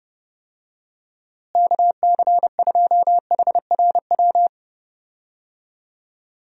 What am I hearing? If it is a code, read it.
Morse code, heard as a keyed tone, about 31 wpm, 700 Hz. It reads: KC2HRW
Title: USA Call Signs - 6 Characters - Repeated 3x - 30wpm